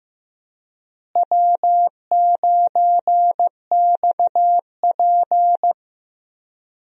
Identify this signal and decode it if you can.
Morse code, heard as a keyed tone, about 15 wpm, 700 Hz. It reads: W9XP
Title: USA Call Signs - 4 Characters - Repeated 3x - 15wpm